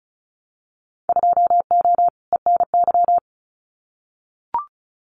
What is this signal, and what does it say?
Morse code, heard as a keyed tone, about 34 wpm, 700 Hz. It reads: 2O ENY
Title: Winter Field Day Exchanges 35wpm